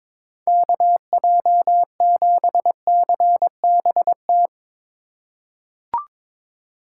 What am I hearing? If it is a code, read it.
Morse code, heard as a keyed tone, about 22 wpm, 700 Hz. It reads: KJ7CBT
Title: USA Call Signs - 6 Characters - 3x Character Spacing - 22wpm